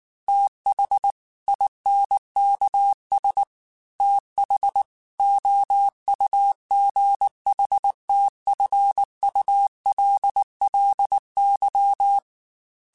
Morse code, heard as a keyed tone, about 19 wpm, 800 Hz. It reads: TH INKS TH OUGHTFULLY